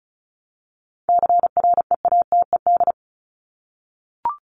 Morse code, heard as a keyed tone, about 34 wpm, 700 Hz. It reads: CREATED
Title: ED Suffix - Top 100 Words with Base Words 35wpm